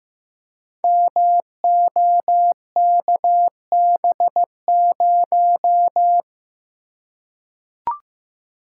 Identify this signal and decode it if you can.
Morse code, heard as a keyed tone, about 15 wpm, 700 Hz. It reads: MOKB0
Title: Code Groups - 5 Letter:Numbers - 15wpm